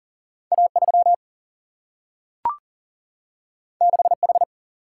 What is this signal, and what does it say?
Morse code, heard as a keyed tone, about 40 wpm, 700 Hz. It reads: A 3 6 H
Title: Single-Letter Single-Number 40wpm